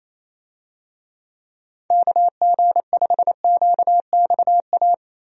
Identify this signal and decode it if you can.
Morse code, heard as a keyed tone, about 28 wpm, 700 Hz. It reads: KG5QXA
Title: USA Call Signs - 6 Characters - Repeated 3x - 28wpm